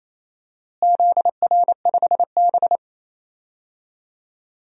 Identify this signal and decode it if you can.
Morse code, heard as a keyed tone, about 28 wpm, 700 Hz. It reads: ZR5B
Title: DXCC-100 Call Signs - 4 Characters - Repeated 3x - 28wpm